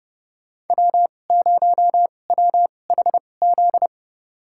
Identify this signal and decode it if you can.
Morse code, heard as a keyed tone, about 30 wpm, spaced as for 22 wpm, 700 Hz. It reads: W0WHZ
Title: USA Call Signs - 5 Characters - 2x Character Spacing - 30wpm